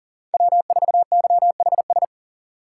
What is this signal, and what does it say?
Morse code, heard as a keyed tone, about 37 wpm, 700 Hz. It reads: W4YHS
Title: USA Call Signs - 5 Characters - 2x Character Spacing - 40wpm